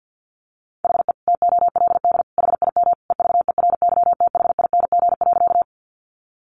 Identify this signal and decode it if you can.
Morse code, heard as a keyed tone, about 50 wpm, 700 Hz. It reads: HE TOLD HIM EVERYTHING.
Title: Sentences from Top 700 Words 50wpm